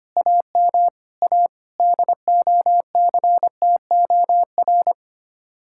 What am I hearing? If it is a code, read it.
Morse code, heard as a keyed tone, about 29 wpm, 700 Hz. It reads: AM A DOCTOR
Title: Speed-Racing Common QSO Phrases - 25wpm